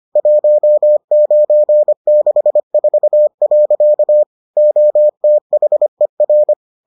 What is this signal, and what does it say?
Morse code, heard as a keyed tone, about 25 wpm, 600 Hz. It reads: 1964. OTHER